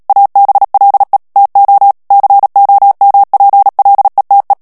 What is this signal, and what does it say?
Morse code, heard as a keyed tone, about 37 wpm, 800 Hz. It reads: ABLE TO COMPLETE
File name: morse mp3 / ANGLAIS texte clair 3 vitesse 35 mots minute